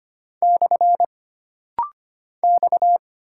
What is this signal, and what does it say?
Morse code, heard as a keyed tone, about 25 wpm, 700 Hz. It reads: / X
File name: Single Character (Letter:Number:Punctuation:Prosign) - Rapid-Fire - 25wpm